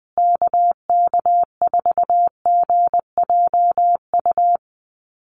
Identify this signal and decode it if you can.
Morse code, heard as a keyed tone, about 20 wpm, 700 Hz. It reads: KK4GJU
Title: USA Call Signs Repeated 3x 20wpm